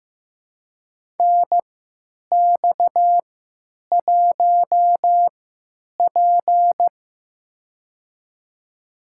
Morse code, heard as a keyed tone, about 15 wpm, spaced as for 9 wpm, 700 Hz. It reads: NX1P
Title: USA Call Signs - 4 Characters - 3x Character Spacing - 15wpm